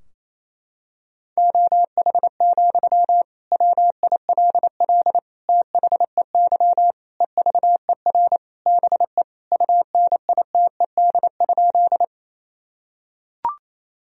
Morse code, heard as a keyed tone, about 28 wpm, 700 Hz. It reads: OH, WILL THEY EVER BE UNITED?
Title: Sentences from Top 1000 Words - Review Part 4 - 28wpm